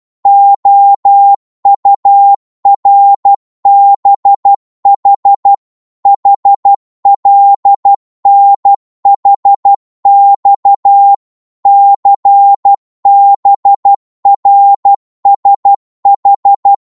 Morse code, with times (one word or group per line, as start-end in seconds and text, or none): 0.3-5.5 OURBH
6.1-11.1 HLNHX
11.7-16.7 CBRSH